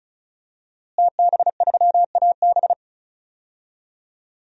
Morse code, heard as a keyed tone, about 34 wpm, 700 Hz. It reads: TB3AB
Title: DXCC-100 Call Signs - 5 Characters - Repeated 3x - 35wpm